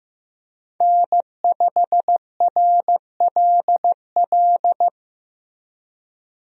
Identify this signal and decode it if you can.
Morse code, heard as a keyed tone, about 15 wpm, 700 Hz. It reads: N5RLL
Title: USA Call Signs - 5 Characters - Repeated 2x - 15wpm